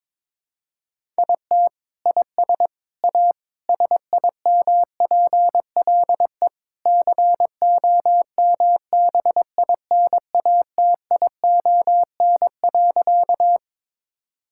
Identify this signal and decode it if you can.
Morse code, heard as a keyed tone, about 22 wpm, 700 Hz. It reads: IT IS A SIMPLE COMBINATION.
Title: Sentences from Top 1000 Words - Review Part 6 - 22wpm